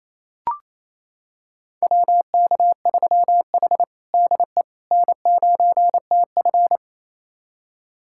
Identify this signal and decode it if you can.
Morse code, heard as a keyed tone, about 29 wpm, 690 Hz. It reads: WK3H DE N9TF
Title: USA Callsign DE Callsign - 4 Characters - 28wpm